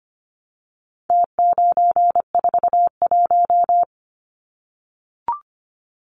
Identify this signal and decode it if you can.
Morse code, heard as a keyed tone, about 25 wpm, 700 Hz. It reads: T941
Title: Numbers - Contest Style - 4-Digit 25wpm